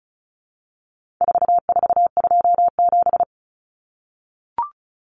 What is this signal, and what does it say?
Morse code, heard as a keyed tone, about 34 wpm, 700 Hz. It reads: 4427
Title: Numbers - 4-Digit 35wpm